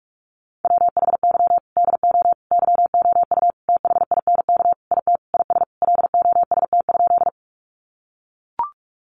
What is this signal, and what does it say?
Morse code, heard as a keyed tone, about 44 wpm, 700 Hz. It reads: WHY DO YOU THINK IT IS LOST?
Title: Sentences from Top 1400 Words 45wpm